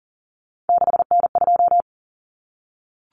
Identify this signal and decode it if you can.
Morse code, heard as a keyed tone, about 40 wpm, 700 Hz. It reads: 6N2
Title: Common Cut Numbers - 3-Digit 40wpm